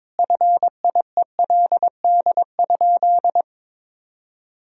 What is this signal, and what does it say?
Morse code, heard as a keyed tone, about 22 wpm, 700 Hz. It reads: FIELD?
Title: Sentences from Top 1000 Words - Review Part 2 - 22wpm